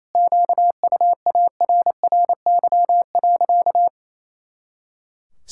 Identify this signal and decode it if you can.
Morse code, heard as a keyed tone, about 27 wpm, 700 Hz. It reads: QUARRY.
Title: Pangram Challenge v2 - 28wpm